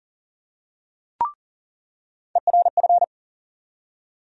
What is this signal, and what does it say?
Morse code, heard as a keyed tone, about 40 wpm, 700 Hz. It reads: ERF
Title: USA Call Sign Postfixes - 3 Letters 40wpm